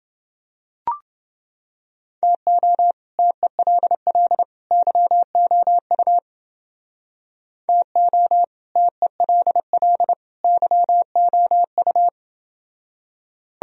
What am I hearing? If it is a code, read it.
Morse code, heard as a keyed tone, about 28 wpm, 700 Hz. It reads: TO TELL YOU TO TELL YOU